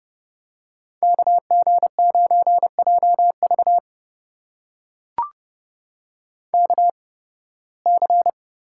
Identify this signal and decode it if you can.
Morse code, heard as a keyed tone, about 30 wpm, spaced as for 20 wpm, 700 Hz. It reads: KG9JV KC
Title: USA Call Signs - 8x Character Spacing - 30wpm